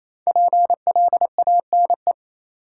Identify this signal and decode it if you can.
Morse code, heard as a keyed tone, about 28 wpm, 700 Hz. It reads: PLANE
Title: Sentences from Top 1000 Words - Review Part 3 - New Word Drill - 28wpm